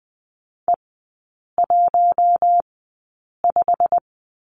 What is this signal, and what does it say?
Morse code, heard as a keyed tone, about 20 wpm, spaced as for 8 wpm, 700 Hz. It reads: E15
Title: 087 Lesson 15e - Number 5 - Three Characters 20wpm